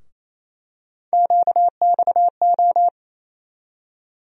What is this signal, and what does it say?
Morse code, heard as a keyed tone, about 28 wpm, 700 Hz. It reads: QXO
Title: USA Call Sign Postfixes - 3 Letters 28wpm